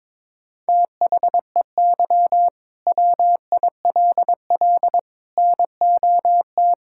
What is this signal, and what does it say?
Morse code, heard as a keyed tone, about 22 wpm, 700 Hz. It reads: THEY WILL NOT
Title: Sentences from Top 1000 Words - Review Part 5 - 22wpm